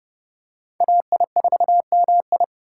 Morse code, heard as a keyed tone, about 30 wpm, 700 Hz. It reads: AI4MI
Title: K1USN SST QSO Practice - 30wpm